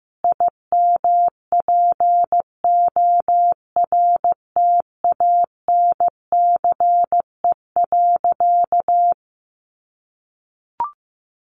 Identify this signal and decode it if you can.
Morse code, heard as a keyed tone, about 15 wpm, 700 Hz. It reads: IMPORTANCE.
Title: Sentences from Top 1200 Words 15wpm